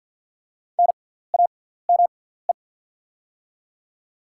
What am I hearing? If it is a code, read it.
Morse code, heard as a keyed tone, about 48 wpm, spaced as for 17 wpm, 690 Hz. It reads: NAME
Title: 6x Character Spacing - Top 200 Words - 50wpm